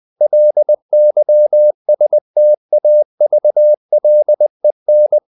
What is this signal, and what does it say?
Morse code, heard as a keyed tone, about 20 wpm, 600 Hz. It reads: LYSTAVLEN
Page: Ferie 20 wpm